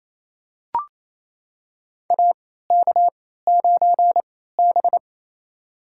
Morse code, heard as a keyed tone, about 28 wpm, spaced as for 16 wpm, 700 Hz. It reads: AK9B
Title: USA Call Signs - 4 Characters - 3x Character Spacing - 28wpm